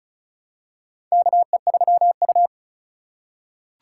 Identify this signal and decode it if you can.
Morse code, heard as a keyed tone, about 34 wpm, 700 Hz. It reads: KE3U
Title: USA Call Signs - 4 Characters - 4x Character Spacing - 35wpm